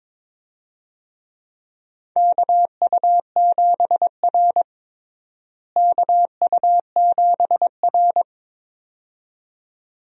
Morse code, heard as a keyed tone, about 22 wpm, 700 Hz. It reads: KU7R KU7R
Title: USA Call Signs - 4 Characters - Repeated 2x - 22wpm